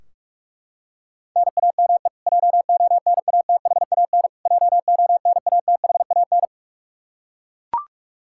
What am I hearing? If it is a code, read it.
Morse code, heard as a keyed tone, about 42 wpm, 700 Hz. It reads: NAME JONATHAN JONATHAN
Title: Practice QSO - Exchange 1 - Beginning - 45wpm